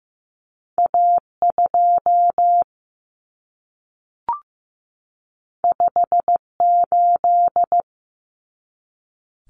Morse code, heard as a keyed tone, about 15 wpm, 700 Hz. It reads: A2 58
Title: Common Cut Numbers - 2-Digit 15wpm